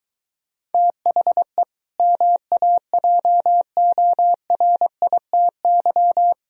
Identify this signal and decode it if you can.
Morse code, heard as a keyed tone, about 24 wpm, 700 Hz. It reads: THE MAJORITY